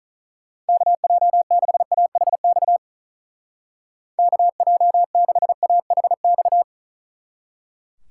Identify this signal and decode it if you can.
Morse code, heard as a keyed tone, about 44 wpm, 700 Hz. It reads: KJ6AHX KJ6AHX